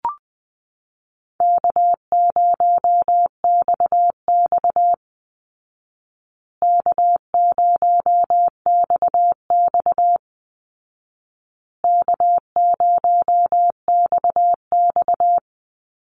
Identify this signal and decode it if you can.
Morse code, heard as a keyed tone, about 20 wpm, 700 Hz. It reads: K0XX K0XX K0XX
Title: Easy Call Signs Repeated 3x - 20wpm